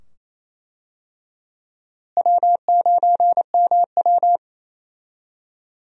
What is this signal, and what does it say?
Morse code, heard as a keyed tone, about 28 wpm, 700 Hz. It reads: W9MW
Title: USA Call Signs - 4 Characters - Repeated 2x - 28wpm